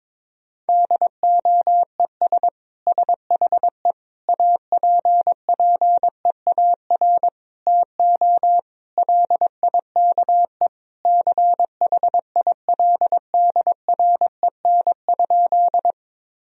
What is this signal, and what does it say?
Morse code, heard as a keyed tone, about 22 wpm, 700 Hz. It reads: DOES SHE APPEAR TO LIKE CHILDREN?
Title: Sentences from Top 1000 Words - Review Part 2 - 22wpm